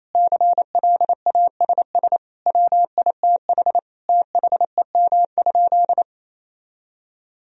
Morse code, heard as a keyed tone, about 28 wpm, 700 Hz. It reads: CLASS WITH THEM?
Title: Sentences from Top 1000 Words - Review Part 2 - 28wpm